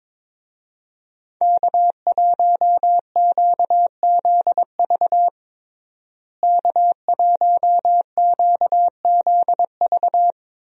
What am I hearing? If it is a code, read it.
Morse code, heard as a keyed tone, about 22 wpm, 700 Hz. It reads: K1QZV K1QZV
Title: USA Call Signs - 5 Characters - Repeated 2x - 22wpm